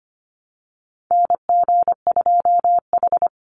Morse code, heard as a keyed tone, about 25 wpm, 700 Hz. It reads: NG2H
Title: Call Signs Repeated 2x - 25wpm